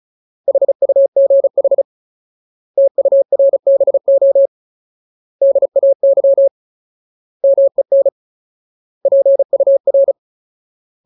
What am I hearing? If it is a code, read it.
Morse code, heard as a keyed tone, about 34 wpm, 550 Hz. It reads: HUGH TURBO DAY MEN PUR
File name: Ultimate Word List / 35WPM Ultimate